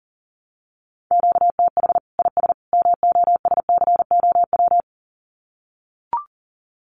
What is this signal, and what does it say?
Morse code, heard as a keyed tone, about 40 wpm, 700 Hz. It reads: QTH IS MOSCOW
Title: Common QSO Phrases 40wpm